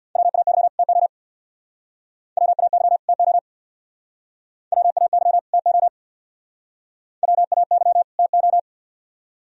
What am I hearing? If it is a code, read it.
Morse code, heard as a keyed tone, about 64 wpm, 700 Hz. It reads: WAY TO WAY TO WAY TO WAY TO